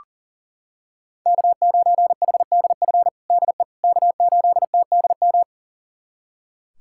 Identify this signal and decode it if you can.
Morse code, heard as a keyed tone, about 37 wpm, 700 Hz. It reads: K9HDF DE K8TDM
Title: USA Callsign DE Callsign - 5 Characters - 40wpm